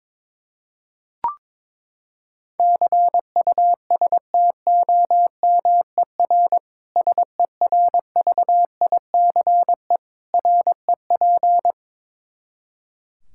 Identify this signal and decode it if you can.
Morse code, heard as a keyed tone, about 22 wpm, 700 Hz. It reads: CUSTOMER SERVICE REP
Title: Practice QSO - Exchange 3 - More Info - 22wpm